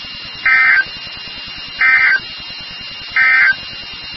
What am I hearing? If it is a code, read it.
Emergency Alert System (SAME) tones, heard as harsh buzzing bursts. It is heard from 0.3 to 3.6 s.
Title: NOAA Weather Radio WXK36